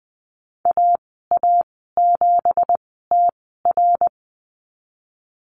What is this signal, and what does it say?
Morse code, heard as a keyed tone, about 20 wpm, spaced as for 15 wpm, 700 Hz. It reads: AA7TR
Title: 173 Lesson 28g - Number 9 - Call Signs 20wpm